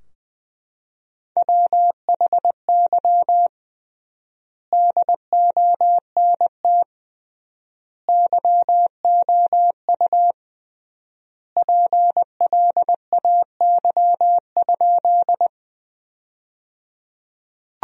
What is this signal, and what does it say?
Morse code, heard as a keyed tone, about 20 wpm, 700 Hz. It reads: WHY DONT YOU PLAY?
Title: 3x Word-Spacing - Sentences from Top 300 Words - 20wpm